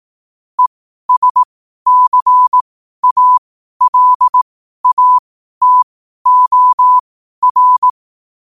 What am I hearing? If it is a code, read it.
Morse code, heard as a keyed tone, about 18 wpm, spaced as for 12 wpm, 1000 Hz. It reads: ESCALATOR
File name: qotdfiles / 12wpm